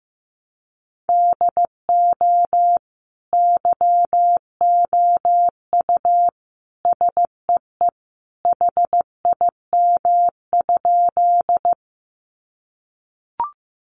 Morse code, heard as a keyed tone, about 15 wpm, 700 Hz. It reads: DO YOU SEE HIM?
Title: Sentences from Top 100 Words 15wpm